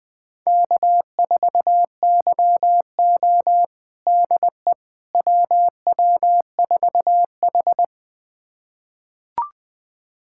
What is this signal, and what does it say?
Morse code, heard as a keyed tone, about 20 wpm, 700 Hz. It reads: K4YO DE WW4H
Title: USA Callsign DE Callsign - 4 Characters - 20wpm